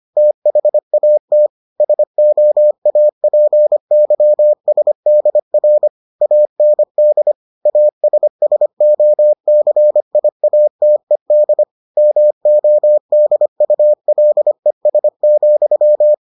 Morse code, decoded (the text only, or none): THAT SOAPYSDR AND ASSOCIATED MODULES,